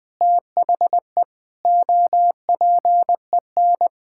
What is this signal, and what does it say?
Morse code, heard as a keyed tone, about 20 wpm, 700 Hz. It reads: THE OPEN
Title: Sentences from Top 1000 Words - Review Part 6 - 20wpm